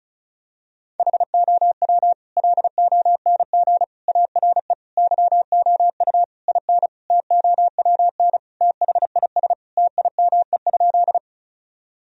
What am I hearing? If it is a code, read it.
Morse code, heard as a keyed tone, about 36 wpm, 700 Hz. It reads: HOW LONG ARE YOU IN TOWN THIS TIME?
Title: Sentences from Top 1000 Words - Review Part 2 - 35wpm